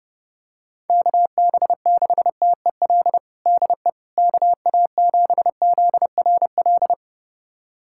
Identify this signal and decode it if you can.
Morse code, heard as a keyed tone, about 30 wpm, 700 Hz. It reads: KB6TEL DE KA7ZRL
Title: USA Callsign DE Callsign 30wpm